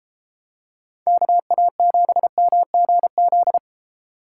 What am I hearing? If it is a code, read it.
Morse code, heard as a keyed tone, about 30 wpm, 700 Hz. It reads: KA7MGZ